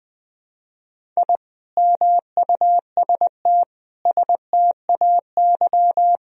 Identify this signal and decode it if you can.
Morse code, heard as a keyed tone, about 19 wpm, 700 Hz. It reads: I MUST STAY
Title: Sentences from Top 1000 Words - Review Part 1 - 20wpm